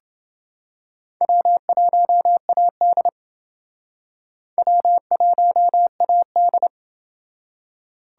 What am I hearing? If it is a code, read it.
Morse code, heard as a keyed tone, about 28 wpm, 700 Hz. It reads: W1AD W1AD